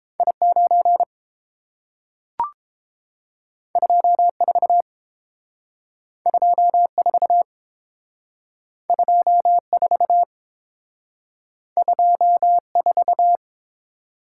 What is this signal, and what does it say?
Morse code, heard as a keyed tone, about 30 wpm, 700 Hz. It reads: I9 24 24 24 24